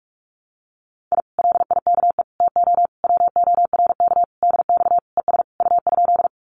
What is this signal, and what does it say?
Morse code, heard as a keyed tone, about 46 wpm, 700 Hz. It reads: I LIKE TO WORK DX ES U?
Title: Common QSO Phrases Encore 45wpm